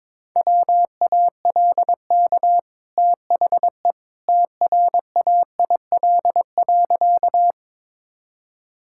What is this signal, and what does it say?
Morse code, heard as a keyed tone, about 22 wpm, 700 Hz. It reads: WALK THE TRAIL.
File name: practice / 22wpm / Easy Sentences with Hints - 22wpm